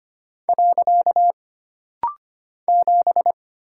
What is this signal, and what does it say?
Morse code, heard as a keyed tone, about 25 wpm, 700 Hz. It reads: . 7
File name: Single Character (Letter:Number:Punctuation:Prosign) - Rapid-Fire - 25wpm